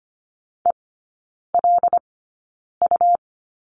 Morse code, heard as a keyed tone, about 25 wpm, spaced as for 9 wpm, 700 Hz. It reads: ELU
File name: course / 25:10wpm / 100 Lesson 17e - Letter u - Three Characters 25wpm